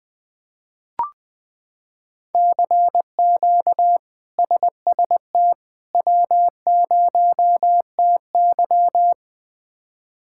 Morse code, heard as a keyed tone, about 21 wpm, 700 Hz. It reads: CQ SST W0TY
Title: K1USN SST QSO Practice - 20wpm